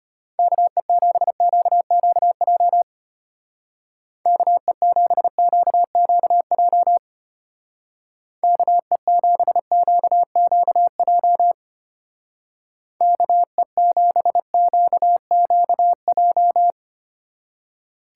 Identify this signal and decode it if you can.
Morse code, heard as a keyed tone, about 32 wpm, 700 Hz. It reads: KE7QQJ KE7QQJ KE7QQJ KE7QQJ